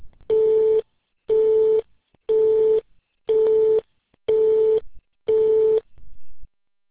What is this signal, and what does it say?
Busy tone (France), 440 Hz.